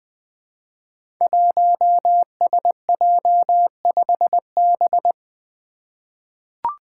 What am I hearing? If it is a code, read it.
Morse code, heard as a keyed tone, about 20 wpm, 700 Hz. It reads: 1SJ5B
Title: Five Challenging Letter-Number Combinations 20wpm